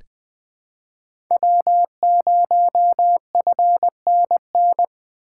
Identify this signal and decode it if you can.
Morse code, heard as a keyed tone, about 20 wpm, 700 Hz. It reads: W0FNN
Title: USA Call Signs - 5 Characters - 2x Character Spacing - 20wpm